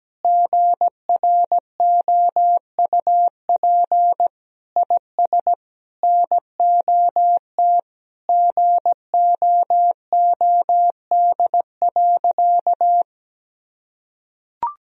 Morse code, heard as a keyed tone, about 17 wpm, 700 Hz. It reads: GROUP IS NOT GOOD.
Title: Sentences from Top 1000 Words - Review Part 1 - 17wpm